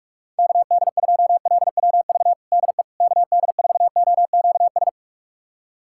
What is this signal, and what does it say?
Morse code, heard as a keyed tone, about 43 wpm, 700 Hz. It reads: KD2LWV DE KD4OQS
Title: USA Callsign DE Callsign - 6 Characters - 45wpm